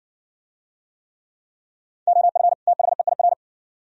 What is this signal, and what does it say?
Morse code, heard as a keyed tone, about 54 wpm, 700 Hz. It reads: OF THEIR